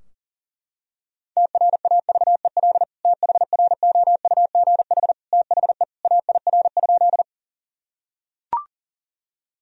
Morse code, heard as a keyed tone, about 40 wpm, 700 Hz. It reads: TRAVEL THROUGH THE AIR?
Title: Sentences from Top 1000 Words - Review Part 2 - 40wpm